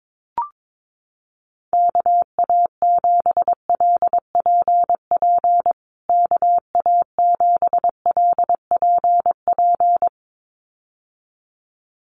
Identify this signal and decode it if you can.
Morse code, heard as a keyed tone, about 22 wpm, 700 Hz. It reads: KA7LPP KA7LPP
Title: USA Call Signs Repeated 2x 22wpm